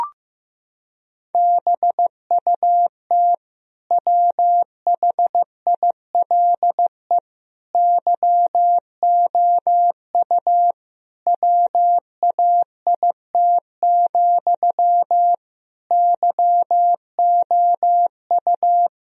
Morse code, heard as a keyed tone, about 15 wpm, 700 Hz. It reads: BUT WHILE YOU WAIT, YOU
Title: Sentences from Top 1000 Words - Review Part 1 - 15wpm